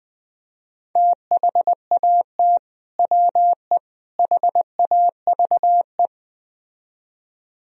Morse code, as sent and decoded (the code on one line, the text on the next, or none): - .... .- - / .-- . / .... .- ...- .
THAT WE HAVE